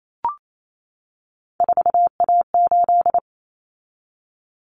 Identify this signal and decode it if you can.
Morse code, heard as a keyed tone, about 28 wpm, 700 Hz. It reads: 4A8
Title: Common Cut Numbers - 3-Digit 28wpm